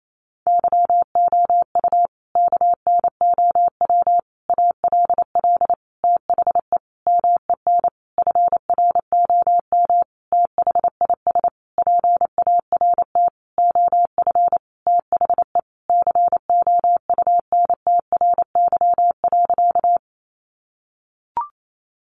Morse code, heard as a keyed tone, about 28 wpm, 700 Hz. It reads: YOU KNOW ALL THE MEN FROM THIS PART OF THE COUNTRY.
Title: Sentences from Top 300 Words 28wpm